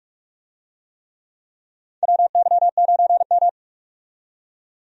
Morse code, heard as a keyed tone, about 43 wpm, 700 Hz. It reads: WY9M
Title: USA Call Signs - 4 Characters - 45wpm